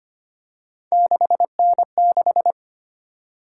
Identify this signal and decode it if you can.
Morse code, heard as a keyed tone, about 25 wpm, 700 Hz. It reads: 6N6
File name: Three Letter-Number Combinations - 25wpm